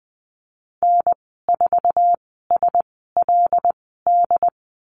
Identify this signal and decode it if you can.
Morse code, heard as a keyed tone, about 20 wpm, spaced as for 15 wpm, 700 Hz. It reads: N4SLD
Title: 082 Lesson 14g - Number 2 - Call Signs 20wpm